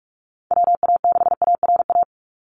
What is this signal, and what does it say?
Morse code, heard as a keyed tone, about 46 wpm, 700 Hz. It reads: WA6ARA
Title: NA SOTA Chaser Callsigns 45wpm